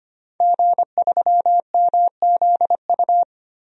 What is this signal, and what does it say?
Morse code, heard as a keyed tone, about 25 wpm, 700 Hz. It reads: G3MZU
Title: DXCC-100 Call Signs - 5 Characters - 25wpm